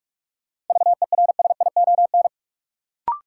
Morse code, heard as a keyed tone, about 46 wpm, 700 Hz. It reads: VERSION
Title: Sentences from Top 1400 Words - New Word Drill Alt - 45wpm